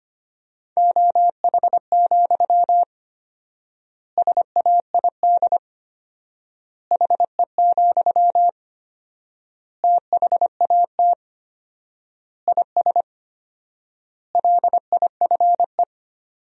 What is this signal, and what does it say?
Morse code, heard as a keyed tone, about 25 wpm, 700 Hz. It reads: OH, SAID HE, THAT IS LIFE
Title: 4x Word-Spacing - Sentences from Top 500 Words - 25wpm